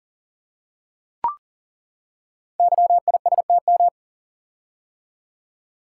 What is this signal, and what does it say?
Morse code, heard as a keyed tone, about 42 wpm, 690 Hz. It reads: YISTM
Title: Code Groups - 5 Letters - 40wpm